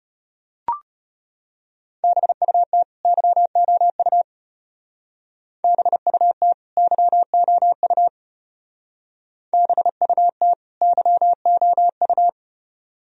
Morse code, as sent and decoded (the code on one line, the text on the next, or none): -... ..- - / -.-- --- ..- / -... ..- - / -.-- --- ..- / -... ..- - / -.-- --- ..-
BUT YOU BUT YOU BUT YOU